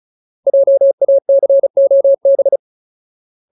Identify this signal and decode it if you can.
Morse code, heard as a keyed tone, about 34 wpm, 550 Hz. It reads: JACOB
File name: Ultimate Word List / 35WPM Ultimate